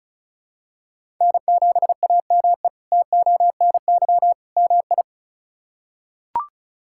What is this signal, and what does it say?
Morse code, heard as a keyed tone, about 34 wpm, 700 Hz. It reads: N7AME TONY MI
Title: K1USN SST QSO Practice - 35wpm